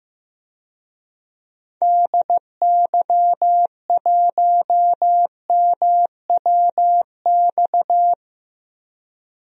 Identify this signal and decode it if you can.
Morse code, heard as a keyed tone, about 15 wpm, 700 Hz. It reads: DY1MWX
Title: DXCC-100 Call Signs - 6 Characters - Repeated 3x - 15wpm